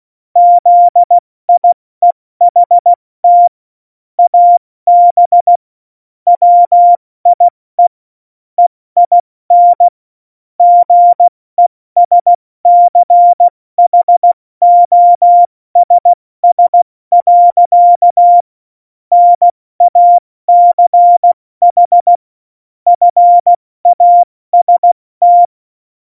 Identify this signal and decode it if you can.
Morse code, heard as a keyed tone, about 16 wpm, 700 Hz. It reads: ZIEHT AB WIE EIN GESCHOSS. NACH FAST
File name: Morsekurs / Witze / Witz 04 August 2020, 70 F80